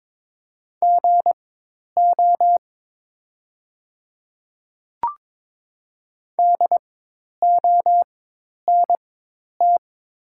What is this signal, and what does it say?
Morse code, heard as a keyed tone, about 21 wpm, spaced as for 10 wpm, 700 Hz. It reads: GO DONT
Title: 4x Character Spacing - Top 500 Words - 22wpm